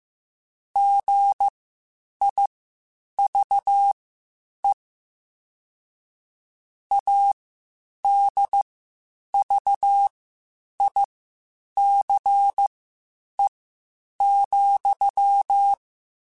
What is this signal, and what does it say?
Morse code, heard as a keyed tone, about 15 wpm, spaced as for 8 wpm, 800 Hz. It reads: GIVE ADVICE,
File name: Quotes Spaced / Quotes Spaced-mp3s / Advice sp @13